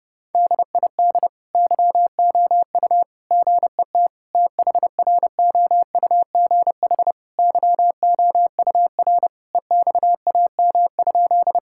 Morse code, heard as a keyed tone, about 30 wpm, 700 Hz. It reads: DID YOU GET THROUGH YOUR EXAM?
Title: Sentences from Top 1000 Words - Review Part 5 - 30wpm